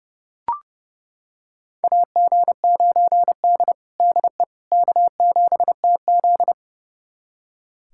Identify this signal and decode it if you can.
Morse code, heard as a keyed tone, about 31 wpm, 700 Hz. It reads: AG9D DE K7TZ
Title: USA Callsign DE Callsign - 4 Characters - 30wpm